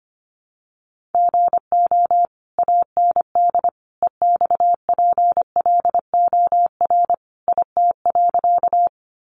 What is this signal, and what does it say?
Morse code, heard as a keyed tone, about 25 wpm, 700 Hz. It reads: GO AND EXPLOR IT.
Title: Sentences from Top 1100 Words 25wpm